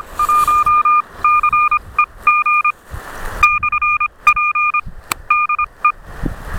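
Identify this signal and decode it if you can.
Morse code, heard as a keyed tone, about 25 wpm, spaced as for 21 wpm, 1210 Hz. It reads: 1CEG /P DE
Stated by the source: QSO between IK4WKU and UA1CEG/p, a portable QRP contact on home-made gear, UA1CEG operating in the forest.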